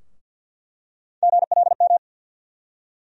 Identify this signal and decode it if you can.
Morse code, heard as a keyed tone, about 53 wpm, 700 Hz. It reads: GLM
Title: USA Call Sign Postfixes - 3 Letters 50wpm